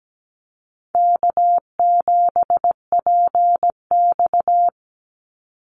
Morse code, heard as a keyed tone, about 17 wpm, 700 Hz. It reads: K7PX
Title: NA SOTA Activator Callsigns 17wpm